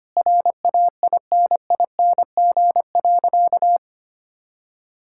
Morse code, heard as a keyed tone, about 25 wpm, 700 Hz. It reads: RAINING.
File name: practice / 25wpm / Sentences with 4-Word Phrases - 25wpm